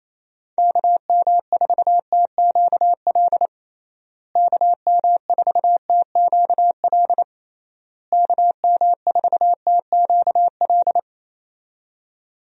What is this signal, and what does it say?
Morse code, heard as a keyed tone, about 28 wpm, 700 Hz. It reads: KM4TQL KM4TQL KM4TQL
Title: USA Call Signs - 6 Characters - Repeated 3x - 28wpm